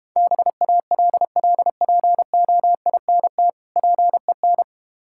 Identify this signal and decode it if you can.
Morse code, heard as a keyed tone, about 32 wpm, 700 Hz. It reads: BALLPOINT PEN